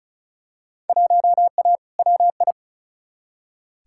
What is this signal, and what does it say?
Morse code, heard as a keyed tone, about 34 wpm, 700 Hz. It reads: 1A WI
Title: Field Day Exchanges - 35wpm